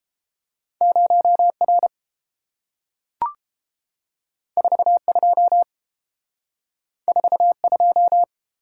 Morse code, heard as a keyed tone, about 30 wpm, 700 Hz. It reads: 0R 42 42